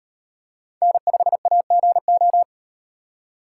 Morse code, heard as a keyed tone, about 35 wpm, 700 Hz. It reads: N5AGO